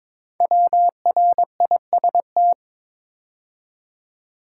Morse code, heard as a keyed tone, about 22 wpm, 700 Hz. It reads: WRIST
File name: 5 Letter Words - First 3 Letters Given - 22wpm